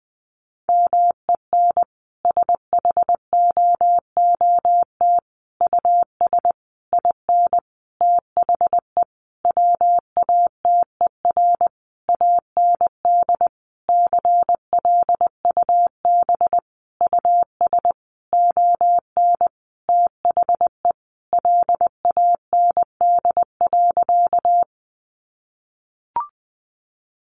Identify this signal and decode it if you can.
Morse code, heard as a keyed tone, about 19 wpm, 700 Hz. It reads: MEN SHOOT US IN THE WATER AND CLUB US ON THE LAND.
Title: Sentences from Top 1200 Words 20wpm